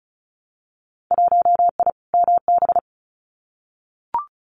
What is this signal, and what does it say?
Morse code, heard as a keyed tone, about 36 wpm, 700 Hz. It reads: 1I MB
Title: Winter Field Day Exchanges Encore 35wpm